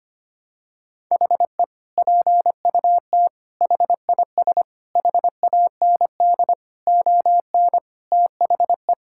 Morse code, heard as a keyed tone, about 25 wpm, 700 Hz. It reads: HE PUT HIS HAND ON THE
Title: Sentences from Top 1000 Words - Review Part 6 - 25wpm